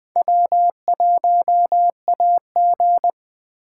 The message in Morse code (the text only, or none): W1AG